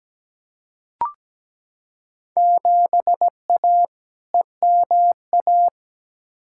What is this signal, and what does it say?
Morse code, heard as a keyed tone, about 17 wpm, 700 Hz. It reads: 7A EMA
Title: Field Day Exchanges Encore - 17wpm